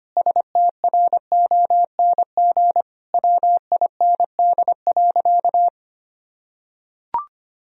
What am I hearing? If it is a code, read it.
Morse code, heard as a keyed tone, about 25 wpm, 700 Hz. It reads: STRONG WIND.
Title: Sentences from Top 1000 Words - Review Part 4 - 25wpm